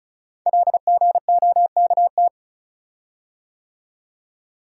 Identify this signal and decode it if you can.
Morse code, heard as a keyed tone, about 36 wpm, 690 Hz. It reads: LGOKT
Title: Code Groups - 5 Letters - 35wpm